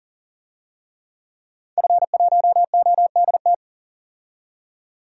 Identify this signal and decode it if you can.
Morse code, heard as a keyed tone, about 38 wpm, 700 Hz. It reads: F1ODT
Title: DXCC-100 Call Signs - 5 Characters - 40wpm